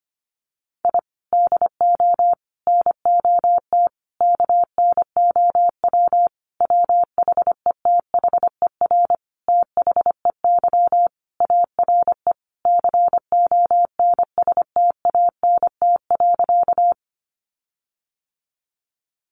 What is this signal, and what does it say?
Morse code, heard as a keyed tone, about 25 wpm, 700 Hz. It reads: I DO NOT KNOW WHETHER THEY ARE CONSTANT.
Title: Sentences from Top 1500 Words 25wpm